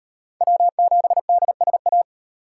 Morse code, heard as a keyed tone, about 35 wpm, 700 Hz. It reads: W7DSA